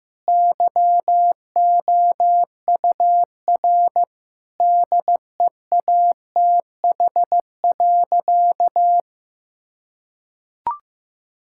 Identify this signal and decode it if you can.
Morse code, heard as a keyed tone, about 15 wpm, 700 Hz. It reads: YOUR DEATH.
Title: Sentences from Top 1600 Words 15wpm